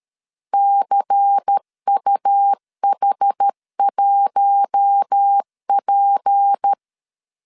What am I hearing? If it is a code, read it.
Morse code, heard as a keyed tone, about 13 wpm, 800 Hz. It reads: CUH1P
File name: LEZIONI / 6 PKX0 / 6 P 12wpm 2033